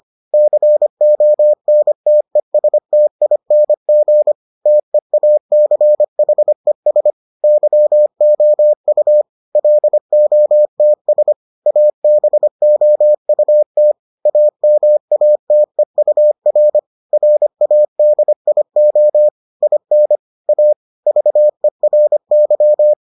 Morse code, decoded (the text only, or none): CONTESTING TEACHES YOU LOTS ABOUT AMATEUR RADIO IN A VERY